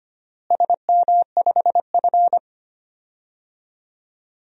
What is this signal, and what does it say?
Morse code, heard as a keyed tone, about 25 wpm, 700 Hz. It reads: SM5F
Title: DXCC-100 Call Signs - 4 Characters - Repeated 2x - 25wpm